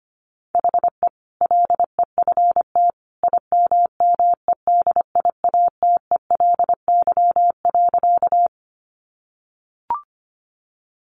Morse code, heard as a keyed tone, about 25 wpm, 700 Hz. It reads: HE LEFT IMMEDIATELY.
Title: Sentences from Top 1100 Words 25wpm